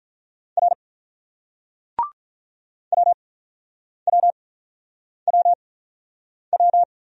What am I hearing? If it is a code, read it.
Morse code, heard as a keyed tone, about 49 wpm, spaced as for 9 wpm, 700 Hz. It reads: R WWWW